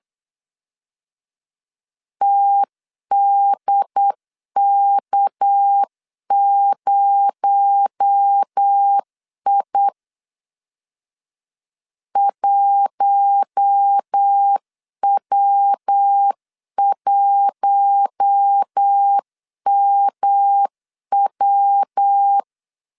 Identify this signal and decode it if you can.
Morse code, heard as a keyed tone, about 8 wpm, 800 Hz. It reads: TDK0I 1W1MW
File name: CW / DET / 1 6 1500C S C 8wpm 5219